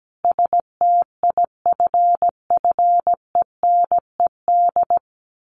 Morse code, heard as a keyed tone, about 17 wpm, 700 Hz. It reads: STIFFENED
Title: A Sound Of Thunder 17wpm